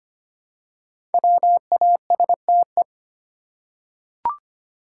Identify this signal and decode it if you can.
Morse code, heard as a keyed tone, about 25 wpm, 700 Hz. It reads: WASTE
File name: 5 Letter Words - First 3 Letters Given - 25wpm